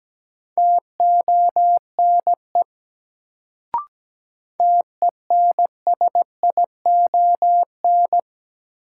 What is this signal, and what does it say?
Morse code, heard as a keyed tone, about 17 wpm, 700 Hz. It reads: TONE TENSION
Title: Sentences from Top 1600 Words - New Word Drill - 17wpm